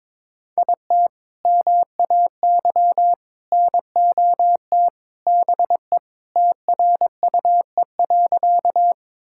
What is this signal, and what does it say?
Morse code, heard as a keyed tone, about 22 wpm, 700 Hz. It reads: IT MAY NOT BE TRUE.
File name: practice / 22wpm / Sentences with 3-Word Phrases - 22wpm